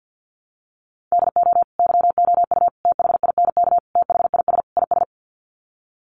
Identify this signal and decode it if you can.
Morse code, heard as a keyed tone, about 50 wpm, 700 Hz. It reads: DO YOU THINK THIS IS
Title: Sets of 5 Words Encore2 50wpm